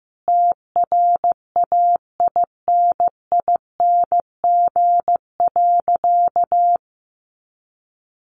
Morse code, heard as a keyed tone, about 15 wpm, 700 Hz. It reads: TRAINING.
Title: Sentences from Top 1400 Words 15wpm